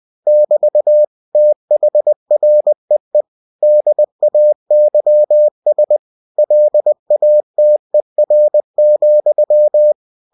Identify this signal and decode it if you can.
Morse code, heard as a keyed tone, about 20 wpm, 600 Hz. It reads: <BT> THREE DAYS LATER,
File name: cw-books / winnie-the-pooh / winnie-20wpm-page-0094